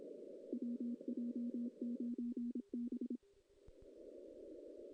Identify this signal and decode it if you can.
Morse code, heard as a keyed tone, about 26 wpm, 240 Hz. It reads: WJ9B
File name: 2016 JAN CW NAQP / 2016 Jan naqp k5ot - 0-5